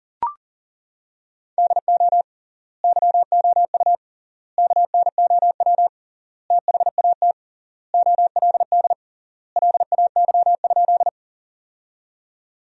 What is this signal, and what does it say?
Morse code, heard as a keyed tone, about 40 wpm, 700 Hz. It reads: DO YOU KNOW THAT OLD LAY?
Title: 3x Word-Spacing - Sentences from Top 400 Words - 40wpm